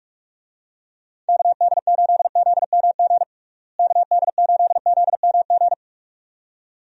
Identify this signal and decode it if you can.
Morse code, heard as a keyed tone, about 45 wpm, 700 Hz. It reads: KD8ZMG KD8ZMG
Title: USA Call Signs - 6 Characters - Repeated 2x - 45wpm